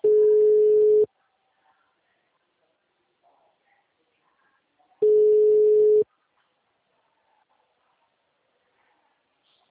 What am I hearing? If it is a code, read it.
Ringback tone (Europe, 425 Hz).
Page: Poupa Card 2